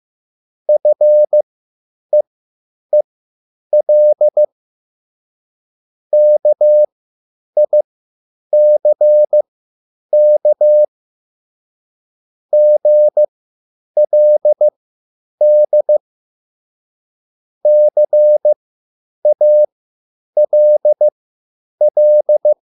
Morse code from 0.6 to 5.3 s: FEEL